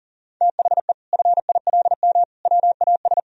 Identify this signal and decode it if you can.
Morse code, heard as a keyed tone, about 40 wpm, 700 Hz. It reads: THE FILM WAS
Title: Sentences from Top 1000 Words - Review Part 3 - 40wpm